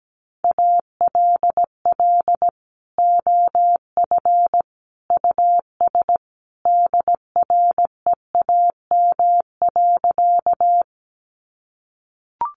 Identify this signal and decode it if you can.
Morse code, heard as a keyed tone, about 17 wpm, 700 Hz. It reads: ALL OF US DREAM.
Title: Sentences from Top 1500 Words 17wpm